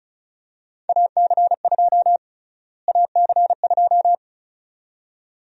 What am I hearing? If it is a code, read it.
Morse code, heard as a keyed tone, about 34 wpm, 700 Hz. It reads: AC2 AC2
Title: USA Call Sign Prefixes - Repeated 2x - 35wpm